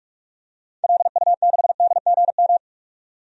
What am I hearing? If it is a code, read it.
Morse code, heard as a keyed tone, about 46 wpm, 700 Hz. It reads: LU6DGM
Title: DXCC-100 Call Signs - 6 Characters - 45wpm